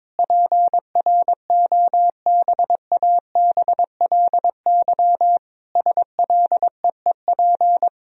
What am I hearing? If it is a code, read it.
Morse code, heard as a keyed tone, about 22 wpm, 700 Hz. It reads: PROBABLY SLEEP
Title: Sentences from Top 1000 Words - Review Part 4 - 22wpm